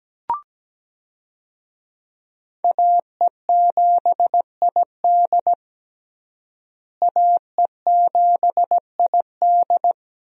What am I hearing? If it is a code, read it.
Morse code, heard as a keyed tone, about 17 wpm, 700 Hz. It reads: AE7ID AE7ID
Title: USA Call Signs - 5 Characters - Repeated 3x - 17wpm